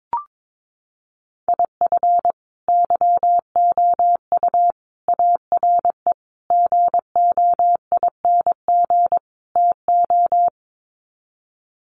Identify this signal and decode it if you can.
Morse code, heard as a keyed tone, about 22 wpm, 700 Hz. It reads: IF YOU ARE GOING TO
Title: Sets of 5 Words Encore2 22wpm